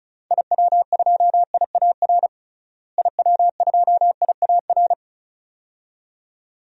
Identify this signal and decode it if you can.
Morse code, heard as a keyed tone, about 36 wpm, 700 Hz. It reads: IW2IAR IW2IAR
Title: DXCC-100 Call Signs - 6 Characters - Repeated 2x - 35wpm